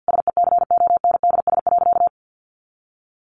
Morse code, heard as a keyed tone, about 49 wpm, 700 Hz. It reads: SECONDS.
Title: A Sound Of Thunder 50wpm